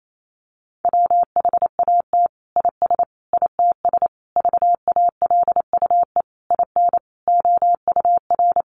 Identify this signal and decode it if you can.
Morse code, heard as a keyed tone, about 28 wpm, 700 Hz. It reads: WHAT IS ITS VALUE IN OUR
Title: Sentences from Top 600 Words 28wpm